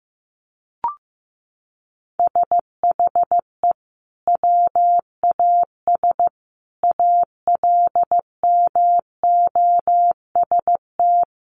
Morse code, heard as a keyed tone, about 14 wpm, 700 Hz. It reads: SHE WAS ALMOST
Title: Sentences from Top 1500 Words 15wpm